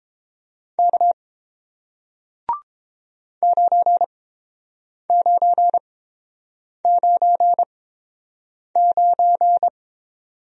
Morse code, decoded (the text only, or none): K 9999